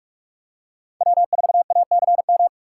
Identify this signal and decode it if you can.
Morse code, heard as a keyed tone, about 41 wpm, 700 Hz. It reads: W4ACM